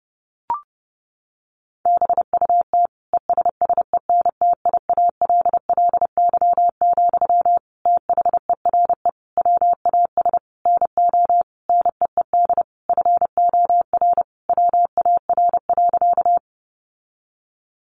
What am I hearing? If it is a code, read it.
Morse code, heard as a keyed tone, about 30 wpm, 700 Hz. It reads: BUT ESSENTIALLY, THERE WAS NO NEED FOR WAR.
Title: Sentences from Top 1300 Words 30wpm